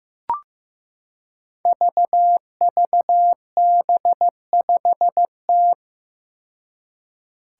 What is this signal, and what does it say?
Morse code, heard as a keyed tone, about 15 wpm, 700 Hz. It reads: VVB5T
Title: Five Challenging Letter-Number Combinations 15wpm